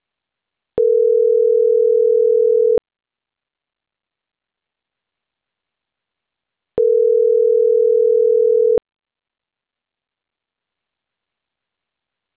Ringback tone (North America), 440 + 480 Hz.